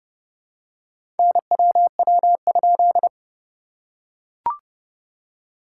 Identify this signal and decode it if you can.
Morse code, heard as a keyed tone, about 31 wpm, 690 Hz. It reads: NWW?
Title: SOTA QSO Practice 30wpm